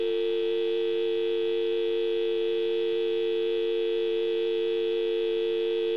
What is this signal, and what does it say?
Dial tone (UK), 350 + 450 Hz, with noise.